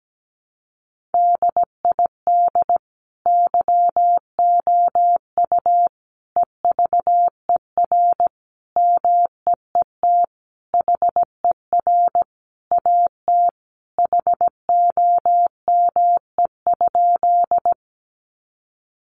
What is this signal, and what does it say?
Morse code, heard as a keyed tone, about 17 wpm, 700 Hz. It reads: DID YOU EVER MEET HER AT HOME?
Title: Sentences from Top 1100 Words 17wpm